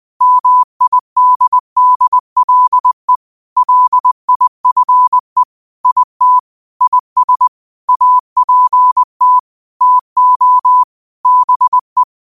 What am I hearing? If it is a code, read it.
Morse code, heard as a keyed tone, about 20 wpm, 1000 Hz. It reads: MIDDLE LIFE IT IS APT TO BE
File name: qotdfiles / 20wpm